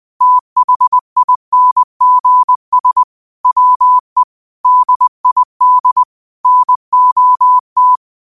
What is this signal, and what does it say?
Morse code, heard as a keyed tone, about 20 wpm, 1000 Hz. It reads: THINGS WE DID NOT